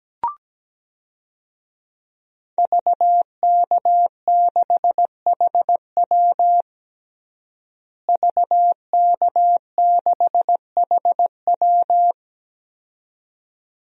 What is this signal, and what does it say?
Morse code, heard as a keyed tone, about 17 wpm, 700 Hz. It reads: VK6HW VK6HW
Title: DXCC-100 Call Signs - 5 Characters - Repeated 2x - 17wpm